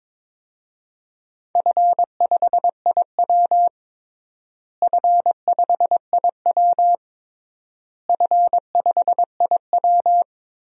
Morse code, heard as a keyed tone, about 22 wpm, 700 Hz. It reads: F5IW F5IW F5IW
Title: DXCC-100 Call Signs - 4 Characters - Repeated 3x - 22wpm